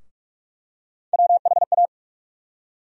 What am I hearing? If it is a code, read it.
Morse code, heard as a keyed tone, about 42 wpm, 700 Hz. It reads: WHA
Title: USA Call Sign Postfixes - 3 Letters 45wpm